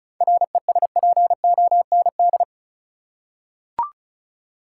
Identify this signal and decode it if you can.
Morse code, heard as a keyed tone, about 36 wpm, 700 Hz. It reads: RESPOND
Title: Seven Letter Words - 35wpm